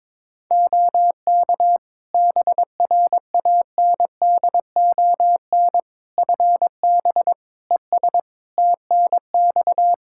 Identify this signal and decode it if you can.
Morse code, heard as a keyed tone, about 21 wpm, 700 Hz. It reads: OK BRANDON FB ES TNX
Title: K1USN SST QSO Practice 22wpm